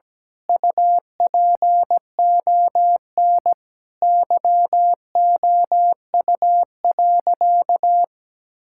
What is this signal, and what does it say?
Morse code, heard as a keyed tone, about 17 wpm, 700 Hz. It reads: UPON YOU.
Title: Sentences from Top 1600 Words 17wpm